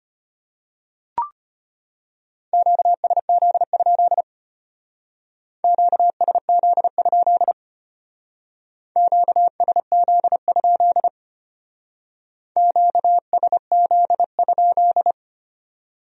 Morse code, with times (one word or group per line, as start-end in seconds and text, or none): 2.5-4.2 QSZ?
5.6-7.5 QSZ?
9.0-11.1 QSZ?
12.6-15.1 QSZ?